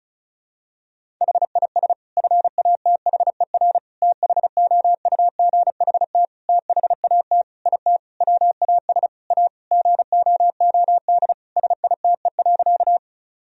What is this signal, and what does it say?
Morse code, heard as a keyed tone, about 36 wpm, 700 Hz. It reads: HIS FATHER THOUGHT THAT IT WAS A GOOD SITE.
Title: Sentences from Top 1000 Words - Review Part 4 - 35wpm